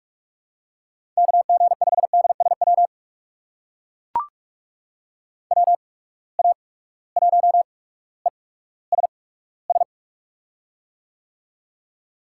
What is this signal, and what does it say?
Morse code, heard as a keyed tone, about 44 wpm, spaced as for 19 wpm, 700 Hz. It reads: KG5DSW WA1ESS
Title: USA Call Signs - 6 Characters - 8x Character Spacing - 45wpm